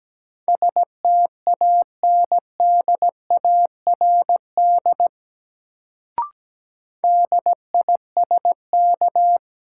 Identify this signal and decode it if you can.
Morse code, heard as a keyed tone, about 17 wpm, 700 Hz. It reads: STANDARD DISK
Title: Sentences from Top 1000 Words - Review Part 6 - New Word Drill - 17wpm